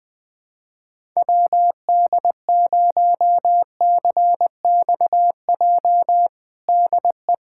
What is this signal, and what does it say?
Morse code, heard as a keyed tone, about 20 wpm, 700 Hz. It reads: WD0CXJ DE
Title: USA Callsign DE Callsign - 6 Characters - 20wpm